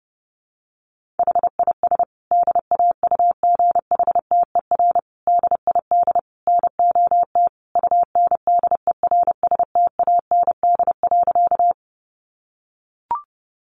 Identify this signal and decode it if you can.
Morse code, heard as a keyed tone, about 30 wpm, 700 Hz. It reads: HIS DAUGHTER DID NOT UNDERSTAND.
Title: Sentences from Top 1500 Words 30wpm